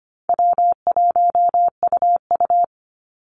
Morse code, heard as a keyed tone, about 25 wpm, 700 Hz. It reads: W1UU
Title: Easy Call Signs Repeated 2x - 25wpm